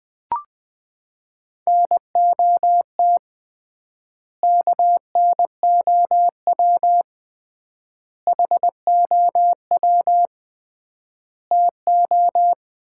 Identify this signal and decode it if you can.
Morse code, heard as a keyed tone, about 20 wpm, 700 Hz. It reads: NOT KNOW HOW TO
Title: 3x Word-Spacing - Sets of 4 Words - 20wpm